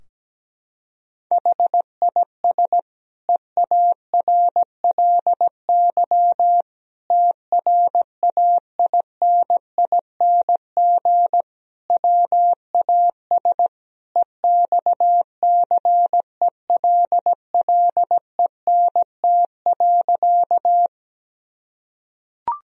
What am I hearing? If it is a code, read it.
Morse code, heard as a keyed tone, about 17 wpm, 700 Hz. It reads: HIS EARLY TRAINING WAS EXCELLENT.
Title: Sentences from Top 1000 Words - Review Part 4 - 17wpm